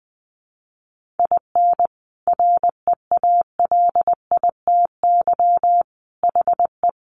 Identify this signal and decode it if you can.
Morse code, heard as a keyed tone, about 19 wpm, 700 Hz. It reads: IN REALITY HE
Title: Sentences from Top 1200 Words 20wpm